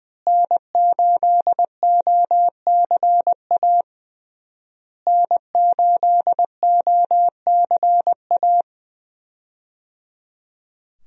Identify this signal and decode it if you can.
Morse code, heard as a keyed tone, about 20 wpm, 700 Hz. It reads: N8OCA N8OCA
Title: USA Call Signs - 5 Characters - Repeated 3x - 20wpm